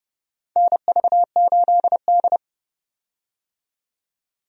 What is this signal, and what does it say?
Morse code, heard as a keyed tone, about 30 wpm, 700 Hz. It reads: NV8D
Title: USA Call Signs - 4 Characters - 30wpm